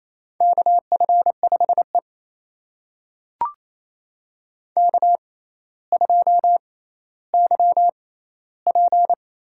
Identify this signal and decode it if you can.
Morse code, heard as a keyed tone, about 28 wpm, spaced as for 14 wpm, 700 Hz. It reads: KF5E K2YP
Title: USA Call Signs - 4 Characters - 6x Character Spacing - 28wpm